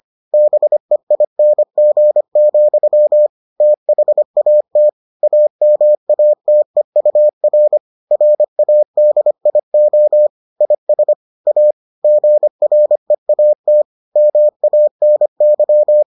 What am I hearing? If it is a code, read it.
Morse code, heard as a keyed tone, about 25 wpm, 600 Hz. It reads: BEING, THAT AMATEUR RADIO IS A GREAT MANY